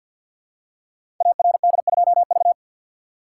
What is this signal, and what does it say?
Morse code, heard as a keyed tone, about 51 wpm, 700 Hz. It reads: ARD2V
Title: Code Groups - 5 Letter:Numbers - 50wpm